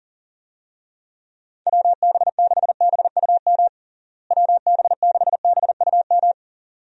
Morse code, heard as a keyed tone, about 40 wpm, 700 Hz. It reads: WB6BUM WB6BUM
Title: USA Call Signs - 6 Characters - Repeated 2x - 40wpm